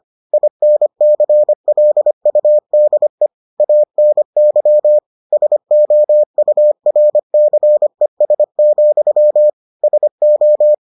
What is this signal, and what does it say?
Morse code, heard as a keyed tone, about 25 wpm, 600 Hz. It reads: INCLUDE ANY SOURCES, SO